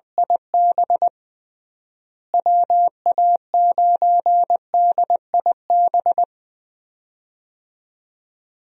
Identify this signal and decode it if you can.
Morse code, heard as a keyed tone, about 20 wpm, 700 Hz. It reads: IB WA9DIB
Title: USA Call Signs - 6 Characters - Repeated 2x - 20wpm